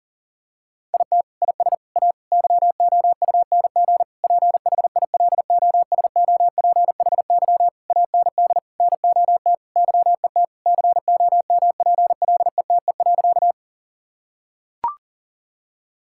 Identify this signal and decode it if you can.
Morse code, heard as a keyed tone, about 40 wpm, 700 Hz. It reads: IT IS A YOUNG PHILOSOPHY AND NOT YET COMPLETE.
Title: Sentences from Top 1000 Words - Review Part 6 - 40wpm